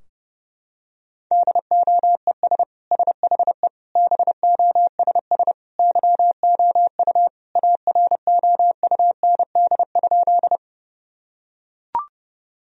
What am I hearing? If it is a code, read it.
Morse code, heard as a keyed tone, about 30 wpm, 700 Hz. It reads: DOES SHE BOSS YOU AROUND?
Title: Sentences from Top 1000 Words - Review Part 4 - 30wpm